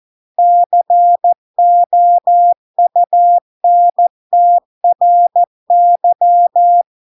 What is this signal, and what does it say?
Morse code, heard as a keyed tone, about 14 wpm, 700 Hz. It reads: COUNTRY